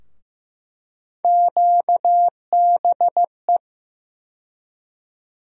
Morse code, heard as a keyed tone, about 15 wpm, 700 Hz. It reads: QBE
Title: USA Call Sign Postfixes - 3 Letters 15wpm